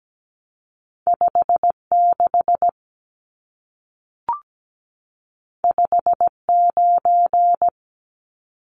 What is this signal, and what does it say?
Morse code, heard as a keyed tone, about 17 wpm, 700 Hz. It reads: 56 59
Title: Numbers - 2-Digit 17wpm